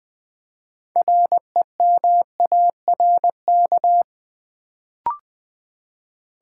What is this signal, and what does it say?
Morse code, heard as a keyed tone, about 20 wpm, 700 Hz. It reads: REMARK
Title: Build A Word - 2 Letters At A Time - 6-Letter Words - 20wpm